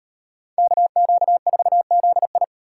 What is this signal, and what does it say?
Morse code, heard as a keyed tone, about 35 wpm, 700 Hz. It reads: KQ4ZI